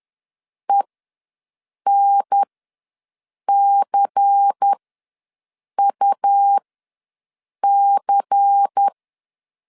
Morse code, heard as a keyed tone, about 10 wpm, spaced as for 6 wpm, 800 Hz. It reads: ENCUC
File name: LEZIONI / 5 LBVC / 5 LBVC ws 10wpm 3733